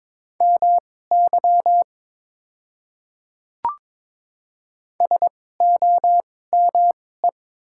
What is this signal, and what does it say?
Morse code, heard as a keyed tone, about 22 wpm, spaced as for 16 wpm, 700 Hz. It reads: MY SOME
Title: 2x Character Spacing - Top 200 Words - 22wpm